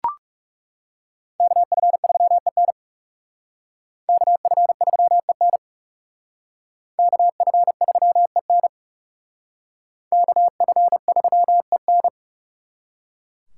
Morse code, heard as a keyed tone, about 37 wpm, 700 Hz. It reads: KF3EN KF3EN KF3EN KF3EN
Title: Speed Racing - USA 2x2 Call Signs 30wpm